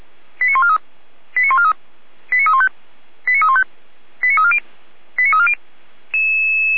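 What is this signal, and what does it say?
Five-tone selective call (ZVEI), heard as quick runs of five tones, tones of about 70 ms. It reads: 89143 89143 89315 89315 89340 89340
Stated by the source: ZVEI five-tone selective call received by SDR on channel 455, recorded for feuerwehr-orlamuende.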